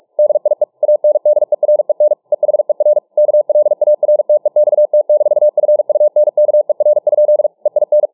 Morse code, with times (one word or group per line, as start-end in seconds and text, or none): 0.2-0.6 DIE
0.8-2.1 ANDEREN
2.3-3.0 EHER
3.2-7.5 KLARTEXT-FUNKER?
7.7-8.1 EIN